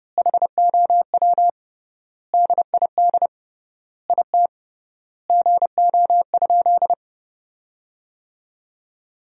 Morse code, heard as a keyed tone, about 30 wpm, 700 Hz. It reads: HOW DID IT GO?
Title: 3x Word-Spacing - Sentences from Top 200 Words - 30wpm